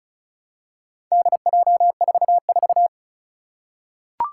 Morse code, heard as a keyed tone, about 36 wpm, 700 Hz. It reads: DJ44
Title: Four Challenging Letter-Number Combinations 35wpm